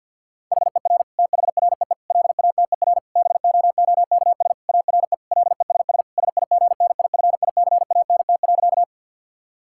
Morse code, heard as a keyed tone, about 52 wpm, 700 Hz. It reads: HER THREE LATER BOOKS ARE LESS SIGNIFICANT.
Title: Sentences from Top 1000 Words - Review Part 6 - 50wpm